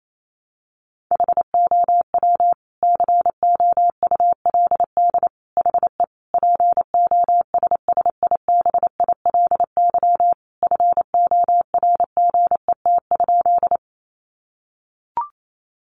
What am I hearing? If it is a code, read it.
Morse code, heard as a keyed tone, about 28 wpm, 700 Hz. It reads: HOW COULD HE POSSIBILY FORGET?
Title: Sentences from Top 1300 Words 28wpm